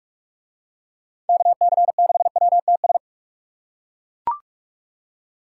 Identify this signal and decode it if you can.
Morse code, heard as a keyed tone, about 40 wpm, 700 Hz. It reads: KC6WTS